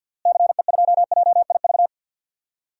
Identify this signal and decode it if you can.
Morse code, heard as a keyed tone, about 52 wpm, 700 Hz. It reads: CE2JIV
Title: DXCC-100 Call Signs - 6 Characters - Repeated 3x - 50wpm